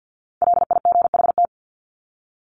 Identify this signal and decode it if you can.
Morse code, heard as a keyed tone, about 48 wpm, 700 Hz. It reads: LIGHT
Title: Top 300 Words 50wpm